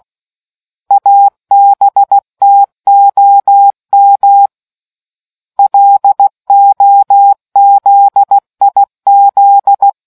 Morse code, heard as a keyed tone, about 16 wpm, 800 Hz. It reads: ABTOM LOZIZ